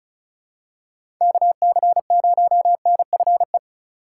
Morse code, heard as a keyed tone, about 34 wpm, 700 Hz. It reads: KC0NFE
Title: USA Call Signs - 6 Characters - 3x Character Spacing - 35wpm